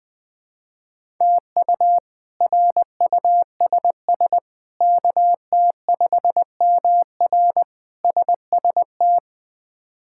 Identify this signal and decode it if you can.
Morse code, heard as a keyed tone, about 21 wpm, 700 Hz. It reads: TU RUSS KT5MR SST
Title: K1USN SST QSO Practice - 20wpm